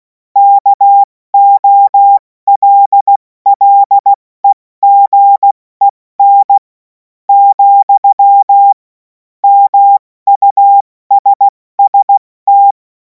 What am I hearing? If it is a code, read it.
Morse code, heard as a keyed tone, about 16 wpm, spaced as for 13 wpm, 800 Hz. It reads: KOLLEGEN, MUSST
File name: Morsekurs / Witze / Witz 17. Februar 2026 70 F80